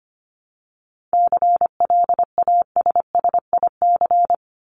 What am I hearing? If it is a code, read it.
Morse code, heard as a keyed tone, about 25 wpm, 700 Hz. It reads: CLASSIC
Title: Top 751-1000 Words 25wpm